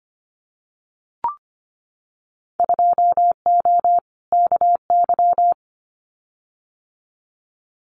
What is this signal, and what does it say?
Morse code, heard as a keyed tone, about 25 wpm, 700 Hz. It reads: 2O KY
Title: Winter Field Day Exchanges 25wpm